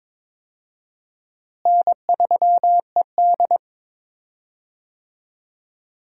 Morse code, heard as a keyed tone, about 21 wpm, 700 Hz. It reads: N3ED
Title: USA Call Signs - 4 Characters - Repeated 3x - 22wpm